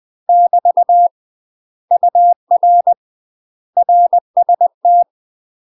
Morse code, heard as a keyed tone, about 20 wpm, 700 Hz. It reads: <BT> UR RST
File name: UK MASTER QSOs TRAINING / 20WPM / UK MASTER QSOs 0066